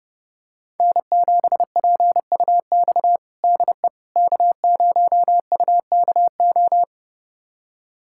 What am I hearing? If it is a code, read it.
Morse code, heard as a keyed tone, about 32 wpm, 700 Hz. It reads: N7PUX DE K0UKO
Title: USA Callsign DE Callsign - 5 Characters - 30wpm